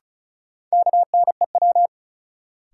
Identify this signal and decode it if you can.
Morse code, heard as a keyed tone, about 36 wpm, 700 Hz. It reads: KNEW
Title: Sentences from Top 400 Words - New Word Drill - 35wpm